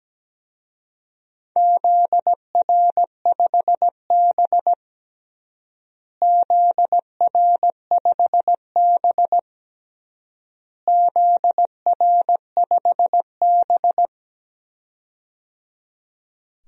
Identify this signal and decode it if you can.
Morse code, heard as a keyed tone, about 17 wpm, 700 Hz. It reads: ZR5B ZR5B ZR5B
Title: DXCC-100 Call Signs - 4 Characters - Repeated 3x - 17wpm